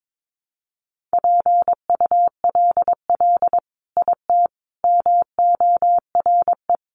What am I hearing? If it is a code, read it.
Morse code, heard as a keyed tone, about 22 wpm, 700 Hz. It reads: PULL IT MORE
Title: Sentences from Top 400 Words 22wpm